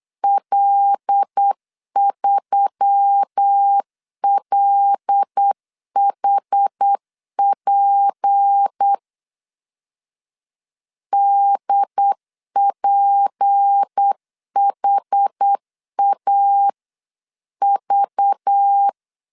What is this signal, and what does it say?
Morse code, heard as a keyed tone, about 8 wpm, 800 Hz. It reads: L3LHP DPHAV
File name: LEZIONI / 6 PKX0 / 6 P 8wpm 3025